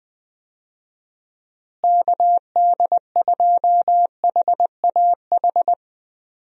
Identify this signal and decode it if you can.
Morse code, heard as a keyed tone, about 21 wpm, 700 Hz. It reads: KD2HAH
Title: USA Call Signs - 6 Characters - Repeated 3x - 20wpm